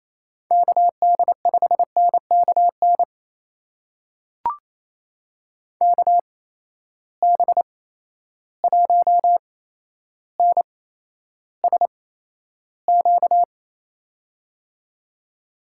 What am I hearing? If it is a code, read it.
Morse code, heard as a keyed tone, about 28 wpm, spaced as for 12 wpm, 700 Hz. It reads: KD5NKN KB1NSQ
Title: USA Call Signs - 6 Characters - 8x Character Spacing - 28wpm